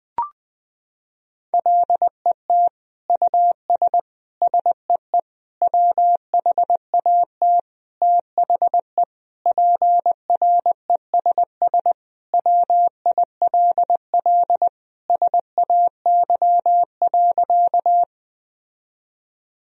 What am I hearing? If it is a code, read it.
Morse code, heard as a keyed tone, about 20 wpm, 700 Hz. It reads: LET US SEE WHAT THE PRESS WILL SAY.
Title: Sentences from Top 1000 Words - Review Part 1 - 20wpm